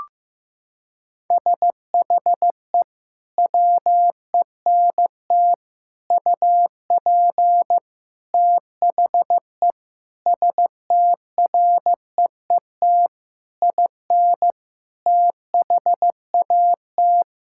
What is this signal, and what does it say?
Morse code, heard as a keyed tone, about 16 wpm, 700 Hz. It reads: SHE WENT UP THE STREET IN THAT
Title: Sentences from Top 1000 Words - Review Part 5 - 15wpm